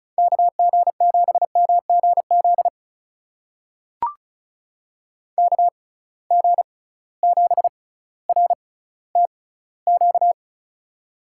Morse code, heard as a keyed tone, about 36 wpm, spaced as for 18 wpm, 700 Hz. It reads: KG7MGZ KG7RTQ
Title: USA Call Signs - 6x Character Spacing - 35wpm